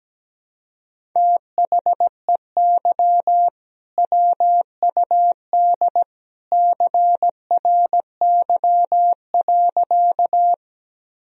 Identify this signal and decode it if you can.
Morse code, heard as a keyed tone, about 17 wpm, 700 Hz. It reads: THEY WUD CRY.
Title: Sentences from Top 500 Words and Abbreviations 17wpm